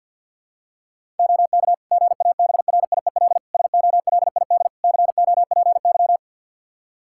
Morse code, heard as a keyed tone, about 52 wpm, 700 Hz. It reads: OK GABRIEL SOLID COPY
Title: Practice QSO - Exchange 3 - More Info - 50wpm